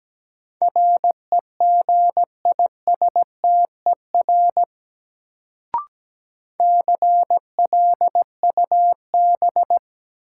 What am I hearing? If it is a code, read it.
Morse code, heard as a keyed tone, about 17 wpm, 700 Hz. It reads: REGISTER CLUB
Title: Sentences from Top 1200 Words - New Word Drill - 17wpm